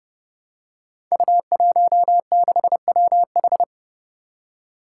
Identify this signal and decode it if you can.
Morse code, heard as a keyed tone, about 30 wpm, 700 Hz. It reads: U16WH
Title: Five Challenging Letter-Number Combinations 30wpm